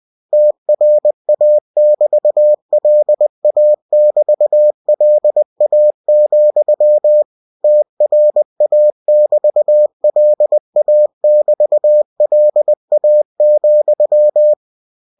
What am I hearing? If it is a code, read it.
Morse code, heard as a keyed tone, about 20 wpm, 600 Hz. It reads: TRA<BT>LA<BT>LA, TRA<BT>LA<BT>LA,
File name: cw-books / winnie-the-pooh / winnie-20wpm-page-0013